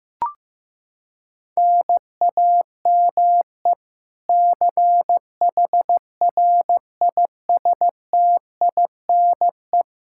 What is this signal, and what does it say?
Morse code, heard as a keyed tone, about 15 wpm, 700 Hz. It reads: NAME CHRISTINE
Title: Practice QSO - Exchange 1 - Beginning - 15wpm